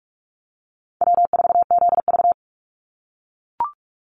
Morse code, heard as a keyed tone, about 44 wpm, 700 Hz. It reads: W4ZV
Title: NA SOTA Chaser Callsigns 45wpm